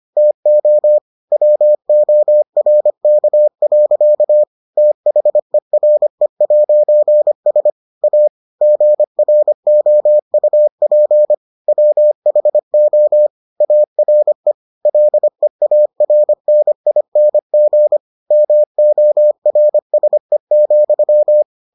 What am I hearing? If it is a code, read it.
Morse code, heard as a keyed tone, about 25 wpm, 600 Hz. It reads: TO WORK. THERE'S A GROUP WHO ARE LEARNING MORSE,